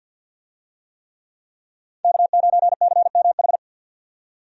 Morse code, heard as a keyed tone, about 49 wpm, 700 Hz. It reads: K9CMH
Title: USA Call Signs - 5 Characters - 50wpm